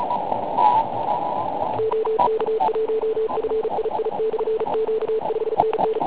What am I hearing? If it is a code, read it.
Morse code, heard as a keyed tone, about 36 wpm, 440 Hz, with noise. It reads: OK1FII/Q 5NN
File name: Pile UP CUC204